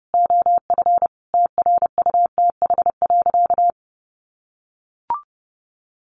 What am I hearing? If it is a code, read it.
Morse code, heard as a keyed tone, about 30 wpm, 700 Hz. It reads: OF TRUTH.
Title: Sentences from Top 1500 Words 30wpm